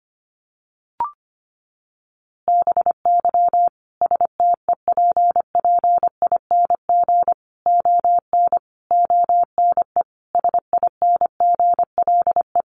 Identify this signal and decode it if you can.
Morse code, heard as a keyed tone, about 25 wpm, 700 Hz. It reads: BY STEPPING ON ONE SINGLE
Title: A Sound Of Thunder 25wpm